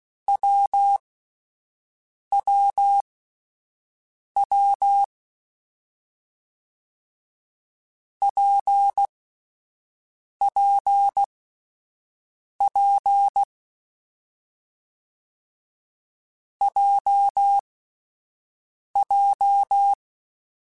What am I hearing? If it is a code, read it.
Morse code, heard as a keyed tone, about 16 wpm, 800 Hz. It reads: W W W P P P J J
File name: Difficult Combos / Difficult Combos-mp3s / WPJ WPJ1 @10